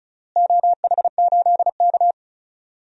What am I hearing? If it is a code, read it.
Morse code, heard as a keyed tone, about 36 wpm, 690 Hz. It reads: OH8K
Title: DXCC-100 Call Signs - 4 Characters - Repeated 3x - 35wpm